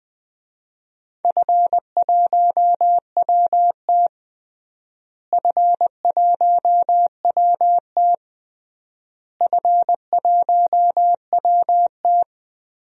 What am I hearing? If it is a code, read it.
Morse code, heard as a keyed tone, about 20 wpm, 700 Hz. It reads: F1WT F1WT F1WT
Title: DXCC-100 Call Signs - 4 Characters - Repeated 3x - 20wpm